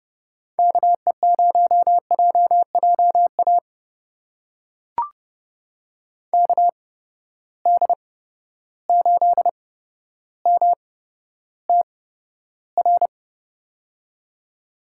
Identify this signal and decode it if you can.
Morse code, heard as a keyed tone, about 32 wpm, spaced as for 13 wpm, 700 Hz. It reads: KE0JJA KD8MTR
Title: USA Call Signs - 8x Character Spacing - 30wpm